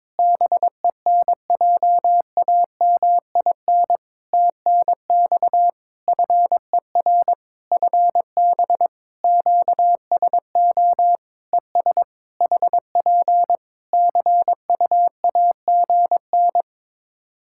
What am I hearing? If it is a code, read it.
Morse code, heard as a keyed tone, about 23 wpm, 700 Hz. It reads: BENJAMIN TNX FER FB QSO ES HP CUAGN
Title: K1USN SST QSO Practice 22wpm